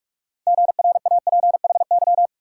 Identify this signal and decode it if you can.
Morse code, heard as a keyed tone, about 59 wpm, 690 Hz. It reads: GRAPHY